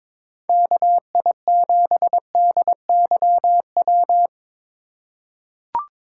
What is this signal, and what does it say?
Morse code, heard as a keyed tone, about 22 wpm, 700 Hz. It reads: KI7DYW
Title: USA Call Signs - 6 Characters - 2x Character Spacing - 22wpm